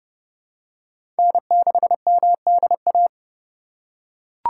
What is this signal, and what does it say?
Morse code, heard as a keyed tone, about 30 wpm, 700 Hz. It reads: N6MDA
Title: USA Call Signs - 5 Characters - 2x Character Spacing - 30wpm